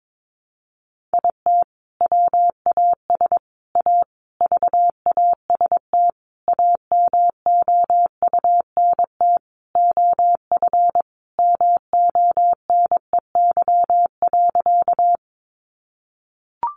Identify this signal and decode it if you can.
Morse code, heard as a keyed tone, about 22 wpm, 700 Hz. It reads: IT WAS A VAST AMOUNT OF MONEY.
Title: Sentences from Top 1300 Words 22wpm